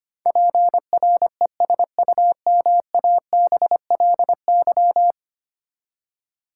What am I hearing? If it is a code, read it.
Morse code, heard as a keyed tone, about 25 wpm, 700 Hz. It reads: PRESUMABLY
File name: Ten Letter Words - 25wpm